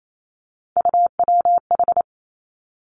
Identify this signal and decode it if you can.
Morse code, heard as a keyed tone, about 28 wpm, 700 Hz. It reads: UWH
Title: Three Challenging Letter-Number Combinations 28wpm